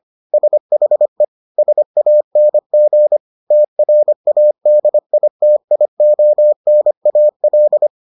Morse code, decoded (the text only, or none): SHE SANG TRADITIONAL